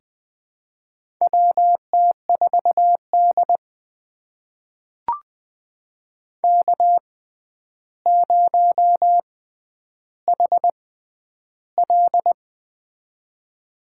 Morse code, heard as a keyed tone, about 20 wpm, spaced as for 10 wpm, 700 Hz. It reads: WT4D K0HL
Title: USA Call Signs - 4 Characters - 6x Character Spacing - 20wpm